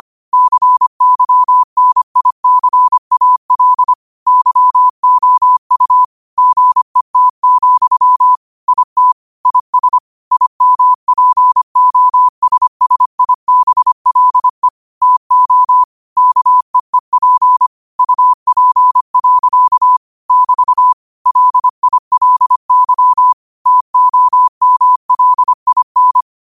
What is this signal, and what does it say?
Morse code, heard as a keyed tone, about 25 wpm, 1000 Hz. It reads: CYNICAL YOU GET, IT IS IMPOSSIBLE TO KEEP UP. <BT> LILY TOMLIN
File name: qotdfiles / 25wpm